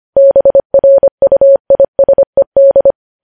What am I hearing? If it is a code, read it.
Morse code, heard as a keyed tone, about 25 wpm, 550 Hz. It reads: BRUISED
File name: podcast / News headlines-1x-25x25